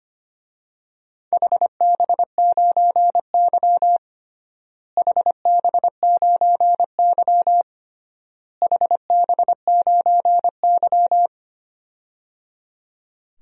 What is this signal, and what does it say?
Morse code, heard as a keyed tone, about 25 wpm, 700 Hz. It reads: HB9Y HB9Y HB9Y
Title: DXCC-100 Call Signs - 4 Characters - Repeated 3x - 25wpm